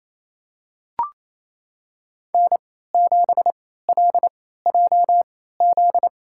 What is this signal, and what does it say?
Morse code, heard as a keyed tone, about 28 wpm, spaced as for 16 wpm, 700 Hz. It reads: N7LJZ
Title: USA Call Signs - 3x Character Spacing - 28wpm